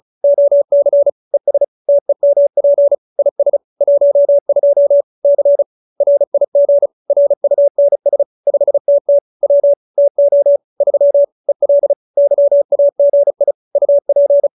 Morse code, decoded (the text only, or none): OC ES TEMP IS 12 C RIG RUNS 5TT W TO 3 EL YAGI UP